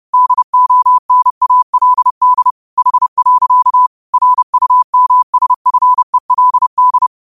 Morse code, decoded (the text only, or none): DONALD H. RUMSFELD